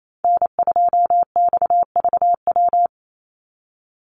Morse code, heard as a keyed tone, about 28 wpm, 700 Hz. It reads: N2XVW
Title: USA Call Signs 28wpm